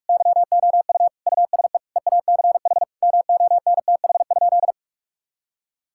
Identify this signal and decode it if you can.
Morse code, heard as a keyed tone, about 46 wpm, 700 Hz. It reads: YOU USE EACH MONTH?
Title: Sentences from Top 1000 Words - Review Part 6 - 45wpm